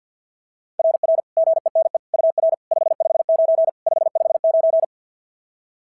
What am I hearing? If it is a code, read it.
Morse code, heard as a keyed tone, about 49 wpm, 690 Hz. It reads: RR GENE UR 559 559
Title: SOTA QSO Practice 50wpm